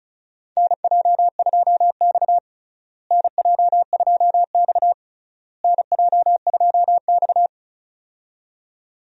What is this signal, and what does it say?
Morse code, heard as a keyed tone, about 36 wpm, 700 Hz. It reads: NJ2X NJ2X NJ2X
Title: USA Call Signs - 4 Characters - Repeated 3x - 35wpm